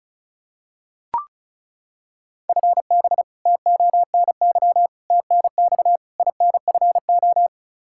Morse code, heard as a keyed tone, about 36 wpm, 700 Hz. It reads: FB TONY TNX INFO
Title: Practice QSO - Exchange 3 - More Info - 35wpm